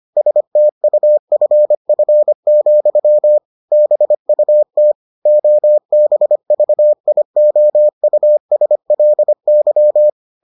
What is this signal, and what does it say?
Morse code, heard as a keyed tone, about 25 wpm, 600 Hz. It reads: STUFF, BUT OBVIOUSLY